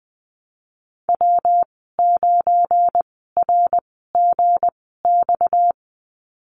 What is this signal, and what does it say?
Morse code, heard as a keyed tone, about 20 wpm, spaced as for 14 wpm, 700 Hz. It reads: W9RGX
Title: 238 Lesson 38g - Letter q - Call Signs 20wpm